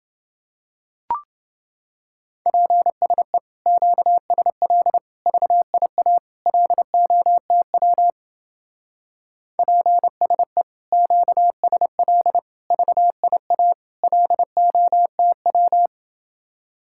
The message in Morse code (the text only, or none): PSE QSL VIA LOTW PSE QSL VIA LOTW